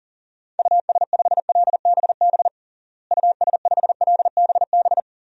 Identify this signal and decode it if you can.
Morse code, heard as a keyed tone, about 37 wpm, 700 Hz. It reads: US5LBB US5LBB
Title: DXCC-100 Call Signs - 6 Characters - Repeated 2x - 40wpm